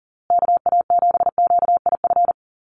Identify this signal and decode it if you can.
Morse code, heard as a keyed tone, about 40 wpm, 700 Hz. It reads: KA7QIF
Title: USA Call Signs Repeated 2x 40wpm